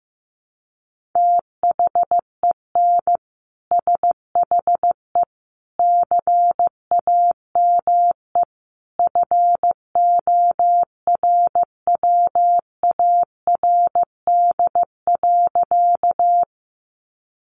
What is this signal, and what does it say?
Morse code, heard as a keyed tone, about 15 wpm, 700 Hz. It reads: THEN SHE CAME FORWARD.
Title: Sentences from Top 1300 Words 15wpm